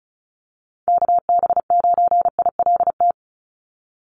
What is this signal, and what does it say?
Morse code, heard as a keyed tone, about 36 wpm, 700 Hz. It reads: KB9ILT
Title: NA SOTA Activator Callsigns 35wpm